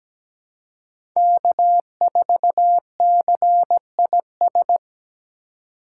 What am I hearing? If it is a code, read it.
Morse code, heard as a keyed tone, about 17 wpm, 700 Hz. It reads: K4CIS
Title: USA Call Signs - 3x Character Spacing - 17wpm